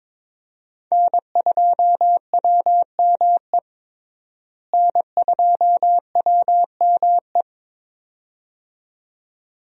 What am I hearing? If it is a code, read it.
Morse code, heard as a keyed tone, about 22 wpm, 700 Hz. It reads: N2WME N2WME
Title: USA Call Signs - 5 Characters - Repeated 2x - 22wpm